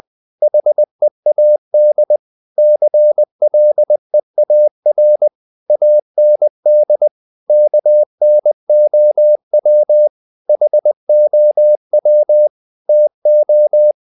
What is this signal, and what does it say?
Morse code, heard as a keyed tone, about 20 wpm, 600 Hz. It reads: HEAD CLEAR AND KNOW HOW TO
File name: the old man and the sea part 2 20wpm